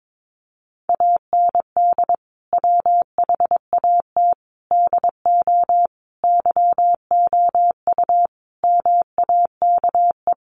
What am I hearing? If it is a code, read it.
Morse code, heard as a keyed tone, about 22 wpm, 700 Hz. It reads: AND WHAT DO YOU MAKE
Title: Sentences from Top 100 Words 22wpm